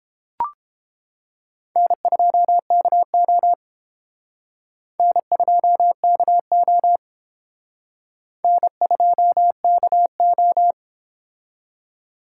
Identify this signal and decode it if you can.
Morse code, heard as a keyed tone, about 30 wpm, 700 Hz. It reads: N2KO N2KO N2KO